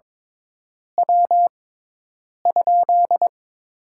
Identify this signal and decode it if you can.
Morse code, heard as a keyed tone, about 22 wpm, 700 Hz. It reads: W ?
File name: Top 20 QSO Elements - 6x Character Spacing - 22wpm